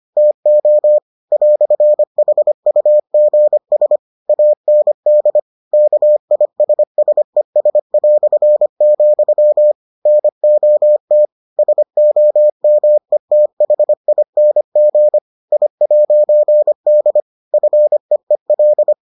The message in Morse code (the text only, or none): TO "HUGS AND KISSES", NOT SOMETHING I'D FEEL